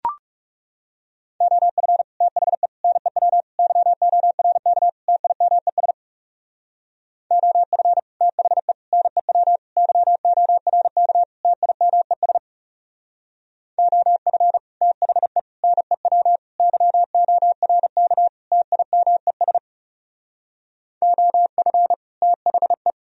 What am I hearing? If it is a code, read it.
Morse code, heard as a keyed tone, about 39 wpm, 700 Hz. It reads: OF THE NEW YORK TIMES OF THE NEW YORK TIMES OF THE NEW YORK TIMES OF THE